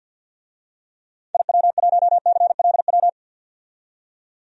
Morse code, heard as a keyed tone, about 46 wpm, 700 Hz. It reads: IW1CLW
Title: DXCC-100 Call Signs - 6 Characters - 50wpm